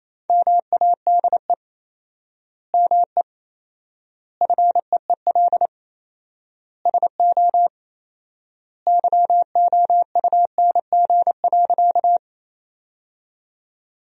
Morse code, heard as a keyed tone, about 28 wpm, 700 Hz. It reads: MADE ME FEEL SO YOUNG.
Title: 4x Word-Spacing - Sentences from Top 400 Words - 28wpm